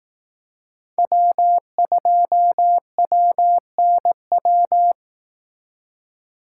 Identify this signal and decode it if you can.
Morse code, heard as a keyed tone, about 18 wpm, 700 Hz. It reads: W2WNW
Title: Speed Racing - USA 1x3 Call Signs 15wpm